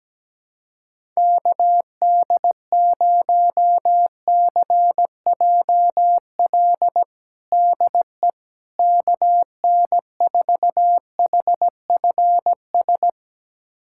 Morse code, heard as a keyed tone, about 17 wpm, 700 Hz. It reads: KD0CJL DE KN4HFS
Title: USA Callsign DE Callsign 17wpm